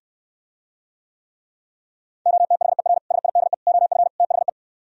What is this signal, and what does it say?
Morse code, heard as a keyed tone, about 69 wpm, 700 Hz. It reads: OTHER SIDE OF THE